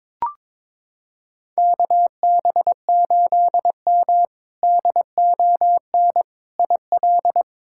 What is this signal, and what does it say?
Morse code, heard as a keyed tone, about 22 wpm, 700 Hz. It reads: KB8M DON IL
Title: K1USN SST QSO Practice - 22wpm